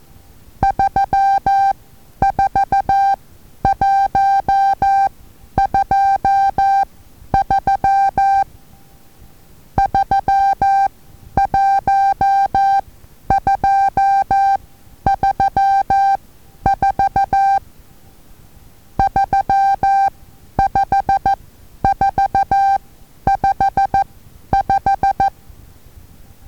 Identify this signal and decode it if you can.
Morse code, heard as a keyed tone, about 14 wpm, spaced as for 10 wpm, 790 Hz. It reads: 34123 31234 35455
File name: Taviro HA6VA A